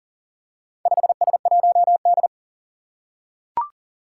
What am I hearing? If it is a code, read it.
Morse code, heard as a keyed tone, about 40 wpm, 700 Hz. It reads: 5S1D
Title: Four Challenging Letter-Number Combinations 40wpm